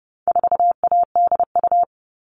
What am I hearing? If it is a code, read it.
Morse code, heard as a keyed tone, about 29 wpm, 700 Hz. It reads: 4ADU
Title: All Cut Numbers - 4-Digit 30wpm